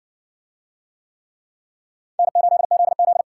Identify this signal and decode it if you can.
Morse code, heard as a keyed tone, about 56 wpm, 700 Hz. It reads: N8ZZ